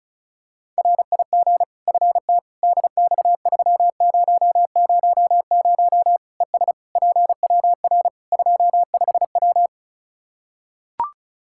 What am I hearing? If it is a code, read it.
Morse code, heard as a keyed tone, about 34 wpm, 700 Hz. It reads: RIG FT DX3000 ES PWR 25W
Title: K1USN SST QSO Practice 35wpm